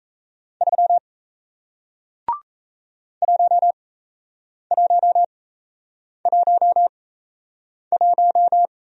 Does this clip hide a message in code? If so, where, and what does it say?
Yes: Morse code from 0.6 to 8.7 s, heard as a keyed tone, about 37 wpm, spaced as for 8 wpm, 700 Hz. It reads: 3 1111